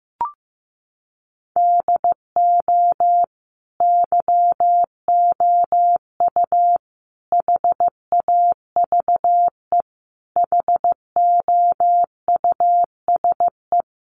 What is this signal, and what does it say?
Morse code, heard as a keyed tone, about 15 wpm, 700 Hz. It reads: DO YOU HAVE HOUSE
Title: Sentences from Top 1100 Words 15wpm